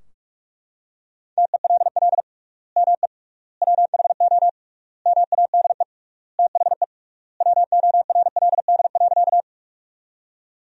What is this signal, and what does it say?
Morse code, heard as a keyed tone, about 41 wpm, 700 Hz. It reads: TELL ME WHO MADE THE WORLD.
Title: 3x Word-Spacing - Sentences from Top 200 Words - 45wpm